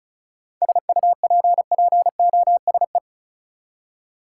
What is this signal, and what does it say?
Morse code, heard as a keyed tone, about 36 wpm, 700 Hz. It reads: SUPPOSE
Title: Seven Letter Words - 35wpm